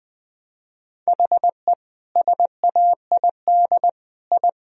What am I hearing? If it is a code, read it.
Morse code, heard as a keyed tone, about 20 wpm, 700 Hz. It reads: HE SAID I
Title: Sentences from Top 1000 Words - Review Part 4 - 20wpm